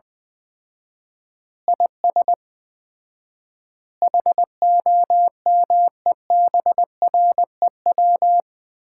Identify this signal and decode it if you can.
Morse code, heard as a keyed tone, about 21 wpm, 700 Hz. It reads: IS HOMEBREW
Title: Common QSO Phrases - 4x Word-Spacing - 20wpm